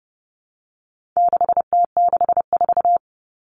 Morse code, heard as a keyed tone, about 30 wpm, 700 Hz. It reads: 6T64
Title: All Cut Numbers - 4-Digit 30wpm